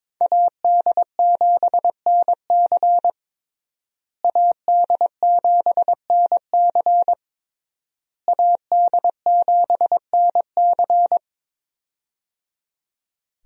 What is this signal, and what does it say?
Morse code, heard as a keyed tone, about 22 wpm, 700 Hz. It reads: AD7NC AD7NC AD7NC
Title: USA Call Signs - 5 Characters - Repeated 3x - 22wpm